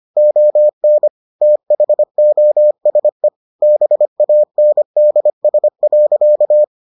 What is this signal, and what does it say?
Morse code, heard as a keyed tone, about 25 wpm, 600 Hz. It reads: ON THOSE BANDS.